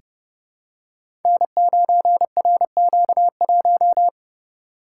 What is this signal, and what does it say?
Morse code, heard as a keyed tone, about 30 wpm, 700 Hz. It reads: N9RQ1
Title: Code Groups - 5 Letter:Numbers - 30wpm